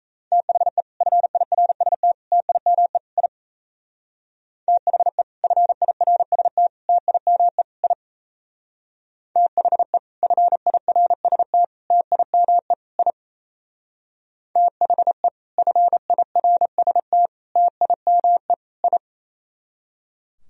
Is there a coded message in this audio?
Yes: Morse code from 0.3 to 19.0 s, heard as a keyed tone, about 35 wpm, 690 Hz. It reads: THE FIRST TIME I THE FIRST TIME I THE FIRST TIME I THE FIRST TIME I